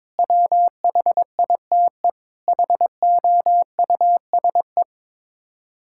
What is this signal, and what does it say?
Morse code, heard as a keyed tone, about 22 wpm, 700 Hz. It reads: WHITE HOUSE
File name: practice / 22wpm / Sets of 8 Words - 22wpm